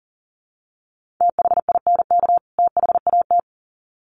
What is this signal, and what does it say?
Morse code, heard as a keyed tone, about 40 wpm, 700 Hz. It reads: THINK THAT
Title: Sets of 2 Words 40wpm